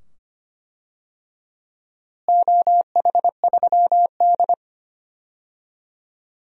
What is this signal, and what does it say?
Morse code, heard as a keyed tone, about 25 wpm, 700 Hz. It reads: OH3D
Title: DXCC-100 Call Signs - 4 Characters - Repeated 2x - 25wpm